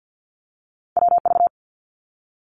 Morse code, heard as a keyed tone, about 50 wpm, 700 Hz. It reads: WV
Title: ARRL Sections 50wpm